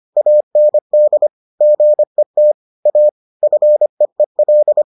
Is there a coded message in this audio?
Yes: Morse code from 0.2 to 4.8 s, heard as a keyed tone, about 25 wpm, 600 Hz. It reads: AND GET A FEEL